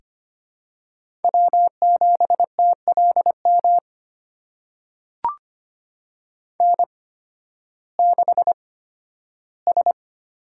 Morse code, heard as a keyed tone, about 25 wpm, spaced as for 13 wpm, 700 Hz. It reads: W7TLM N6S
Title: USA Call Signs - 8x Character Spacing - 25wpm